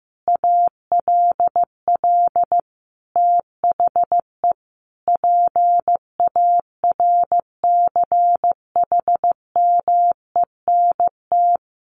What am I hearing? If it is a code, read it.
Morse code, heard as a keyed tone, about 15 wpm, 700 Hz. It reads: ALL THE PARCHMENT
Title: A Sound Of Thunder 15wpm